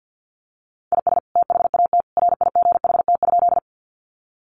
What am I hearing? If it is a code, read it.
Morse code, heard as a keyed tone, about 50 wpm, 700 Hz. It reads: IS THAT RIGHT?
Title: Sentences from Top 200 Words 50wpm